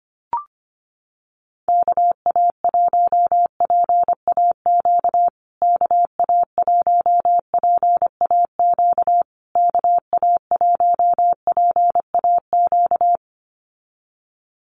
Morse code, heard as a keyed tone, about 25 wpm, 700 Hz. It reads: KA1PAQ KA1PAQ KA1PAQ
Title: USA Call Signs Repeated 3x 25wpm